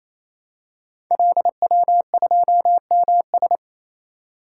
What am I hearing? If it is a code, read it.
Morse code, heard as a keyed tone, about 27 wpm, 700 Hz. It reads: LW2MS
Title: Code Groups - 5 Letter:Numbers - 28wpm